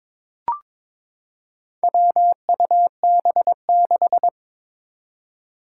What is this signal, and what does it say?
Morse code, heard as a keyed tone, about 22 wpm, 700 Hz. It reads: WUB6
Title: Four Challenging Letter-Number Combinations 22wpm